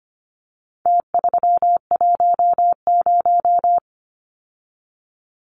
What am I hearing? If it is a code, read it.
Morse code, heard as a keyed tone, about 25 wpm, 700 Hz. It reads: T310
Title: Numbers - Contest Style - 4-Digit 25wpm